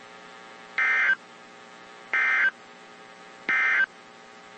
Emergency Alert System (SAME) tones, heard as harsh buzzing bursts. It is heard from 0.7 to 3.9 s.
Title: WZ2523 - NOAA Weather Radio